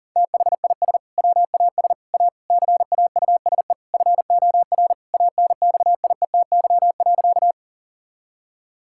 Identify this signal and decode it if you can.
Morse code, heard as a keyed tone, about 40 wpm, 700 Hz. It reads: THIS WAS A CAUSE FOR ANXIETY.
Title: Sentences from Top 1600 Words 40wpm